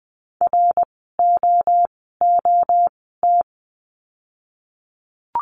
Morse code, heard as a keyed tone, about 20 wpm, spaced as for 14 wpm, 700 Hz. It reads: ROOT
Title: 061 Lesson 11f - Letter h - Words 20wpm